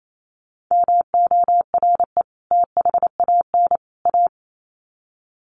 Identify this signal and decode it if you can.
Morse code, heard as a keyed tone, about 28 wpm, 700 Hz. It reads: MORE THAN A
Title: Sets of 3 Words 28wpm